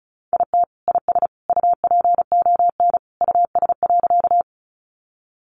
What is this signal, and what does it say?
Morse code, heard as a keyed tone, about 34 wpm, 700 Hz. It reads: IT IS UPON US.
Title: Sentences from Top 1100 Words 35wpm